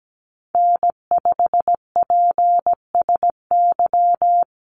Morse code, heard as a keyed tone, about 17 wpm, 700 Hz. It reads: N5PSY
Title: USA Call Signs Repeated 2x 17wpm